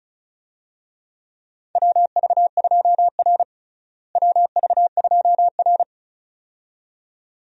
Morse code, heard as a keyed tone, about 36 wpm, 700 Hz. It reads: WV2R WV2R
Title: USA Call Signs - 4 Characters - Repeated 2x - 35wpm